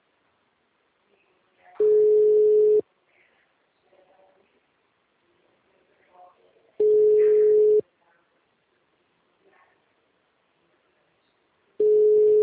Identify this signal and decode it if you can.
Ringback tone (Europe, 425 Hz), with speech.